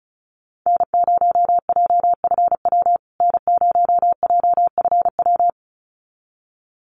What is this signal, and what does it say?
Morse code, heard as a keyed tone, about 34 wpm, 700 Hz. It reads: N0JFW N0JFW
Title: USA Call Signs Repeated 2x 35wpm